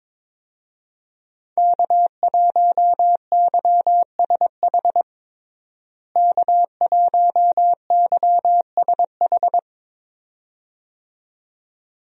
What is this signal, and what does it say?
Morse code, heard as a keyed tone, about 21 wpm, 700 Hz. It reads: K1YSH K1YSH
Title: USA Call Signs - 5 Characters - Repeated 2x - 22wpm